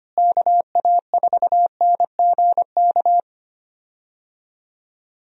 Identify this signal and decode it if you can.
Morse code, heard as a keyed tone, about 25 wpm, 700 Hz. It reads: KA4NGK
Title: USA Call Signs - 6 Characters - Repeated 3x - 25wpm